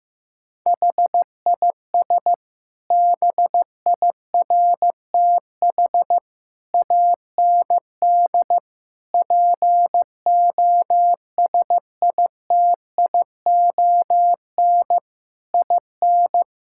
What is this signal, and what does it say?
Morse code, heard as a keyed tone, about 15 wpm, 700 Hz. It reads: HIS BIRTH AND POSITION IN
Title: Sentences from Top 1000 Words - Review Part 5 - 15wpm